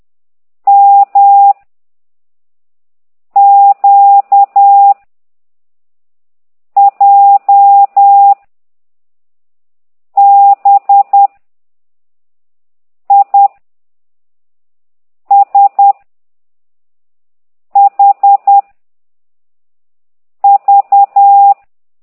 Morse code, heard as a keyed tone, about 10 wpm, 800 Hz. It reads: M Q J B I S H V